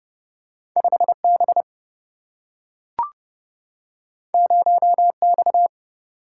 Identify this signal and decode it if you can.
Morse code, heard as a keyed tone, about 30 wpm, 700 Hz. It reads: 5B 0X
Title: Single-Letter Single-Number 30wpm